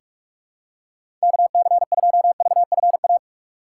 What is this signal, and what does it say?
Morse code, heard as a keyed tone, about 43 wpm, 700 Hz. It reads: KC2VFA
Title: USA Call Signs - 6 Characters - 4x Character Spacing - 45wpm